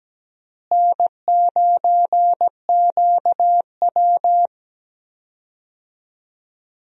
Morse code, heard as a keyed tone, about 17 wpm, 700 Hz. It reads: N9QW
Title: USA Call Signs - 4 Characters - Repeated 2x - 17wpm